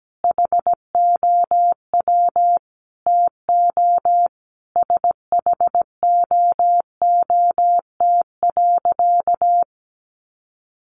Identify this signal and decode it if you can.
Morse code, heard as a keyed tone, about 17 wpm, 700 Hz. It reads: HOW TO SHOOT.
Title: Sentences from Top 1200 Words 17wpm